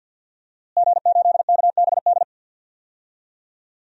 Morse code, heard as a keyed tone, about 51 wpm, 700 Hz. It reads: G8KBD
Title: DXCC-100 Call Signs - 5 Characters - Repeated 2x - 50wpm